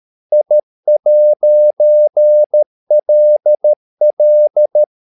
Morse code, heard as a keyed tone, about 13 wpm, 600 Hz. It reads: I'LL